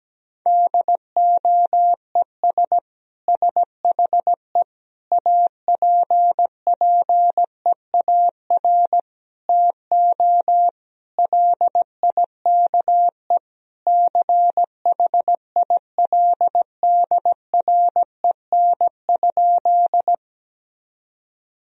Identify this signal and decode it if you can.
Morse code, heard as a keyed tone, about 17 wpm, 700 Hz. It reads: DOES SHE APPEAR TO LIKE CHILDREN?
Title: Sentences from Top 1000 Words - Review Part 2 - 17wpm